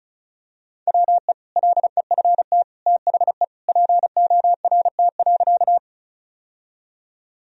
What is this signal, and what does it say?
Morse code, heard as a keyed tone, about 34 wpm, 700 Hz. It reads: WE LEFT THE PORT.
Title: Sentences from Top 1000 Words - Review Part 1 - 35wpm